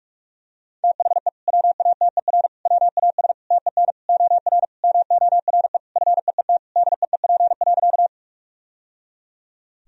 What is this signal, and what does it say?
Morse code, heard as a keyed tone, about 41 wpm, 700 Hz. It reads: THE WATER WAS TEN OR MORE FEET DEEP.
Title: Sentences from Top 1000 Words - Review Part 3 - 45wpm